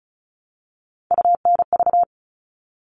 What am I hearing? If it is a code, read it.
Morse code, heard as a keyed tone, about 34 wpm, 700 Hz. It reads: UNV
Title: All Cut Numbers - 3-Digit 35wpm